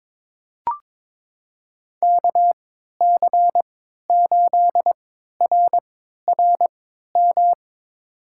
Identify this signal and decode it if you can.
Morse code, heard as a keyed tone, about 23 wpm, spaced as for 13 wpm, 700 Hz. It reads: KC8RRM
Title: USA Call Signs - 6 Characters - 3x Character Spacing - 22wpm